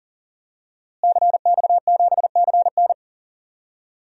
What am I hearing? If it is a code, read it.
Morse code, heard as a keyed tone, about 38 wpm, 700 Hz. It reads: CX7CN
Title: DXCC-100 Call Signs - 5 Characters - Repeated 2x - 40wpm